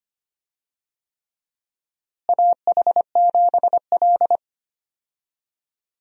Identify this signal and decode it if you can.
Morse code, heard as a keyed tone, about 25 wpm, 700 Hz. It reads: AH7L
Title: USA Call Signs - 4 Characters - Repeated 3x - 25wpm